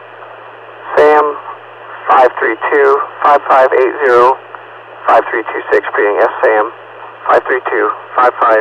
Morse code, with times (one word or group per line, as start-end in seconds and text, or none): none